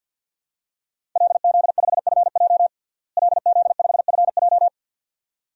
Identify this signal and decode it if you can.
Morse code, heard as a keyed tone, about 51 wpm, 700 Hz. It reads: LZ5FJ LZ5FJ
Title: DXCC-100 Call Signs - 5 Characters - Repeated 2x - 50wpm